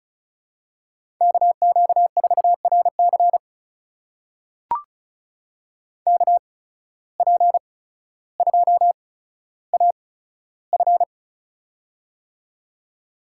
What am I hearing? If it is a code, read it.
Morse code, heard as a keyed tone, about 34 wpm, spaced as for 15 wpm, 700 Hz. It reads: KQ4RC KP2AF
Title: USA Call Signs - 5 Characters - 8x Character Spacing - 35wpm